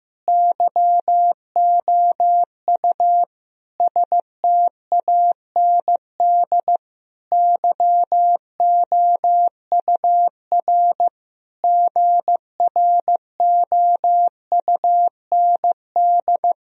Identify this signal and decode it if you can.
Morse code, heard as a keyed tone, about 15 wpm, 700 Hz. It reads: YOU STAND YOUR GROUND
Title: Sentences from Top 1000 Words - Review Part 2 - 15wpm